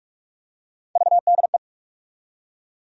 Morse code, heard as a keyed tone, about 48 wpm, 700 Hz. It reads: VDE
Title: USA Call Sign Postfixes - 3 Letters 45wpm